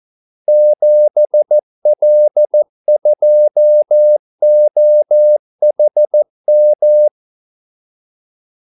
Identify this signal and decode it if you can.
Morse code, heard as a keyed tone, about 14 wpm, 600 Hz. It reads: 7L2OHM
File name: Callsigns-14wpm